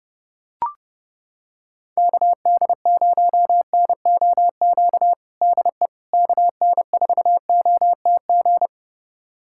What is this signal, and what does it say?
Morse code, heard as a keyed tone, about 32 wpm, 700 Hz. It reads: KD0NOQ DE KN4OTG
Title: USA Callsign DE Callsign - 6 Characters - 30wpm